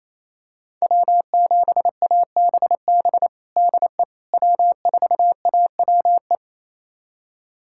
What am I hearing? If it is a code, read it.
Morse code, heard as a keyed tone, about 28 wpm, 700 Hz. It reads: W7ABB DE W4AWE
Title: USA Callsign DE Callsign - 5 Characters - 28wpm